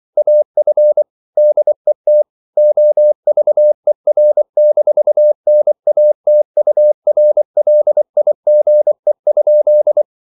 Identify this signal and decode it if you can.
Morse code, heard as a keyed tone, about 24 wpm, 600 Hz. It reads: AF DET OVER-NATURLIGE?
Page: Ferie 23 wpm